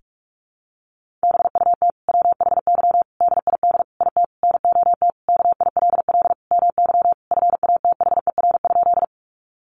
Morse code, heard as a keyed tone, about 44 wpm, 700 Hz. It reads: BUT WHY DID IT NOT KILL MY FATHER?
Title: Sentences from Top 1400 Words 45wpm